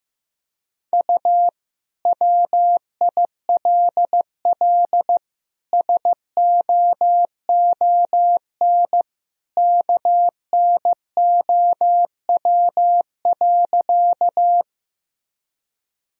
Morse code, heard as a keyed tone, about 15 wpm, 700 Hz. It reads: U WILL SOON KNOW.
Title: Sentences from Top 400 Words and Abbreviations 15wpm